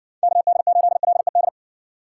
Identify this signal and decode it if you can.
Morse code, heard as a keyed tone, about 60 wpm, 700 Hz. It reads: KD8LED